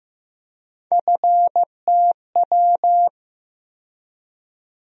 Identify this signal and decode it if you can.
Morse code, heard as a keyed tone, about 15 wpm, 700 Hz. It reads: FTW
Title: USA Call Sign Postfixes - 3 Letters 15wpm